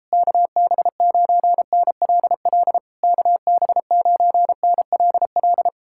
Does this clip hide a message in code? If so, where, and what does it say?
Yes: Morse code from 0.1 to 5.7 s, heard as a keyed tone, about 37 wpm, 700 Hz. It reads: KB9NLL KB9NLL